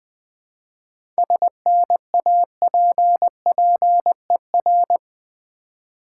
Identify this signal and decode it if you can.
Morse code, heard as a keyed tone, about 20 wpm, 700 Hz. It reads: SNAPPER
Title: Difficult Words with Hints - 20wpm